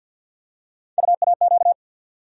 Morse code, heard as a keyed tone, about 51 wpm, 700 Hz. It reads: UAQ
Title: USA Call Sign Postfixes - 3 Letters 50wpm